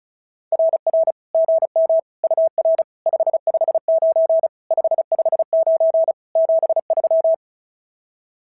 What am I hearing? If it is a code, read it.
Morse code, heard as a keyed tone, about 34 wpm, 660 Hz. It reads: RR GM UR 559 559 73
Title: SOTA QSO Practice 35wpm